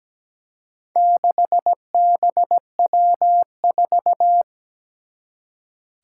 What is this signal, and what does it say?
Morse code, heard as a keyed tone, about 17 wpm, 700 Hz. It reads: 6BW4
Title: Four Challenging Letter-Number Combinations 17wpm